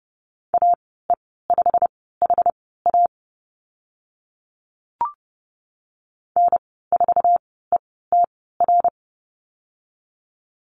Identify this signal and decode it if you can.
Morse code, heard as a keyed tone, about 30 wpm, spaced as for 17 wpm, 700 Hz. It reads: AE5HA N4ETR
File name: Lesson 15g - Number 5 - Call Signs 30wpm